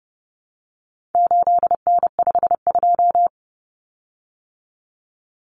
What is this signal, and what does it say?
Morse code, heard as a keyed tone, about 30 wpm, 700 Hz. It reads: 8N52
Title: Common Cut Numbers - 4-Digit 30wpm